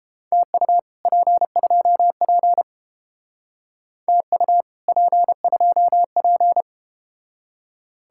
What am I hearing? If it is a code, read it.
Morse code, heard as a keyed tone, about 30 wpm, 700 Hz. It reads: TU P2P TU P2P